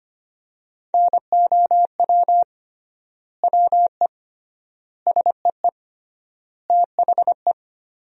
Morse code, heard as a keyed tone, about 25 wpm, 700 Hz. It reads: NOW WE SEE THE
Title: 3x Word-Spacing - Sentences from Top 200 Words - 25wpm